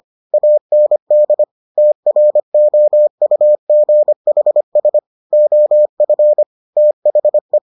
Morse code, heard as a keyed tone, about 25 wpm, 600 Hz. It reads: AND TROUGHS OF THE